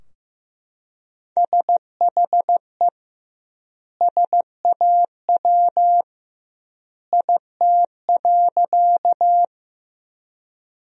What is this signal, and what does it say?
Morse code, heard as a keyed tone, about 15 wpm, 700 Hz. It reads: SHE SAW IT.
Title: 2x Word-Spacing - Sentences from Top 300 Words - 15wpm